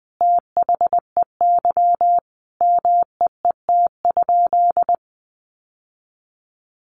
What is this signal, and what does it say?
Morse code, heard as a keyed tone, about 20 wpm, 700 Hz. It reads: THEY MEET?
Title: Sentences from Top 1100 Words 20wpm